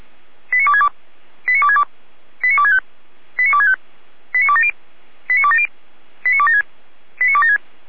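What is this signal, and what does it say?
Five-tone selective call (ZVEI), heard as quick runs of five tones, tones of about 70 ms. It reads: 89262 89262 89265 89265 89270 89270 89276 89276
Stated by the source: ZVEI five-tone call received by SDR.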